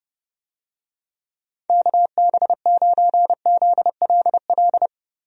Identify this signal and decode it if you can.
Morse code, heard as a keyed tone, about 31 wpm, 700 Hz. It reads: KB9ZLL
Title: USA Call Signs - 6 Characters - Repeated 3x - 30wpm